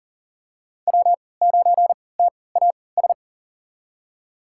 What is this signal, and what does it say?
Morse code, heard as a keyed tone, about 38 wpm, spaced as for 22 wpm, 700 Hz. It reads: W9TAS
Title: USA Call Signs - 5 Characters - 3x Character Spacing - 40wpm